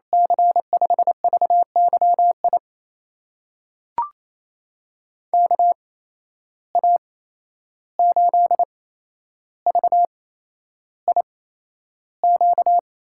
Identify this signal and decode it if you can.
Morse code, heard as a keyed tone, about 28 wpm, spaced as for 11 wpm, 700 Hz. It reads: C5VYI KA8VIQ
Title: USA Call Signs - 6 Characters - 8x Character Spacing - 28wpm